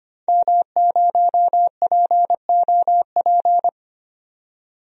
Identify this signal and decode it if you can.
Morse code, heard as a keyed tone, about 25 wpm, 700 Hz. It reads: M0POP
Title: DXCC-100 Call Signs - 5 Characters - 25wpm